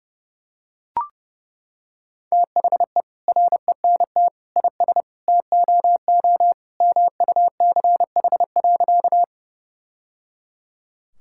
Morse code, heard as a keyed tone, about 30 wpm, 700 Hz. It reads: THE RENT IS TOO MUCH.
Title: Sentences from Top 1000 Words - Review Part 5 - 30wpm